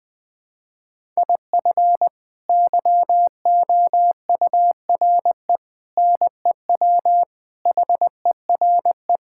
Morse code, heard as a keyed tone, about 20 wpm, 700 Hz. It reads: IF YOURE NEW HERE
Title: Sets of 8 Words - 20wpm